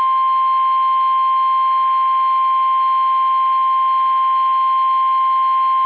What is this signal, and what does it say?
Emergency Alert System (SAME) tones (the 1050 Hz attention tone). It is heard from 0.7 to 5.6 s.